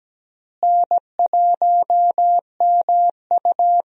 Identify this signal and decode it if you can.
Morse code, heard as a keyed tone, about 17 wpm, 700 Hz. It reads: N1MU
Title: USA Call Signs - 4 Characters - Repeated 2x - 17wpm